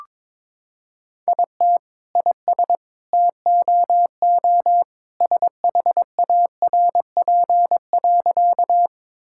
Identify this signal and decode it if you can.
Morse code, heard as a keyed tone, about 22 wpm, 700 Hz. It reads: IT IS TOO SHARP.
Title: Sentences from Top 1600 Words 22wpm